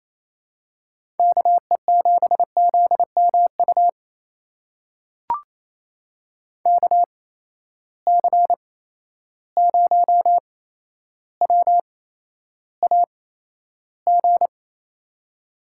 Morse code, heard as a keyed tone, about 28 wpm, spaced as for 12 wpm, 700 Hz. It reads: KE7ZMU KC0WAG
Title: USA Call Signs - 8x Character Spacing - 28wpm